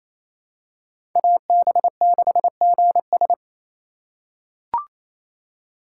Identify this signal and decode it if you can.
Morse code, heard as a keyed tone, about 28 wpm, 700 Hz. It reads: AB6GS
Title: USA Call Signs - 5 Characters - 2x Character Spacing - 28wpm